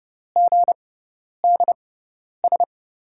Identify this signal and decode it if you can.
Morse code, heard as a keyed tone, about 30 wpm, spaced as for 10 wpm, 700 Hz. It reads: GDS
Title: USA Call Signs - 6 Characters - 6x Character Spacing - 30wpm